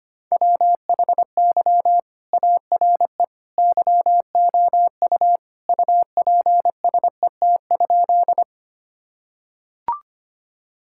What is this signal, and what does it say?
Morse code, heard as a keyed tone, about 25 wpm, 700 Hz. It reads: WHY ARE YOU UPSET?
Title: Sentences from Top 1000 Words - Review Part 5 - 25wpm